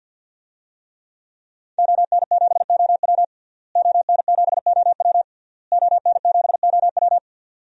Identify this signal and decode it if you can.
Morse code, heard as a keyed tone, about 52 wpm, 700 Hz. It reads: ON7OW ON7OW ON7OW
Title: DXCC-100 Call Signs - 5 Characters - Repeated 3x - 50wpm